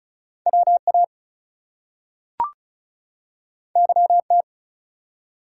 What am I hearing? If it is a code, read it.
Morse code, heard as a keyed tone, about 36 wpm, 700 Hz. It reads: WA YT
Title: US States and Canadian Provinces 35wpm